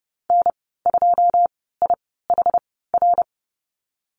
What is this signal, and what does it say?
Morse code, heard as a keyed tone, about 30 wpm, spaced as for 17 wpm, 700 Hz. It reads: N2IHR
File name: Lesson 14g - Number 2 - Call Signs 30wpm